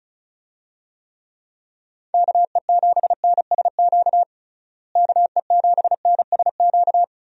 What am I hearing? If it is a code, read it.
Morse code, heard as a keyed tone, about 36 wpm, 700 Hz. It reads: KE7NSQ KE7NSQ
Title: USA Call Signs - 6 Characters - Repeated 3x - 35wpm